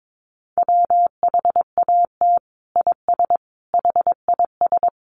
Morse code, heard as a keyed tone, about 22 wpm, 700 Hz. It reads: WHAT IS HIS
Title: Sentences from Top 1000 Words 22wpm